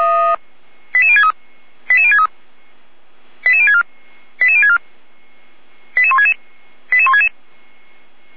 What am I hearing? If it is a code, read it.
Five-tone selective call (ZVEI), heard as quick runs of five tones, tones of about 70 ms. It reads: 88063 88063 88064 88064 88166 88166